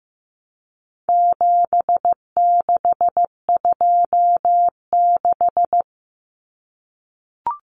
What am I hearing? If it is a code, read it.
Morse code, heard as a keyed tone, about 15 wpm, 700 Hz. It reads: 7626
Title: Numbers - 4-Digit 15wpm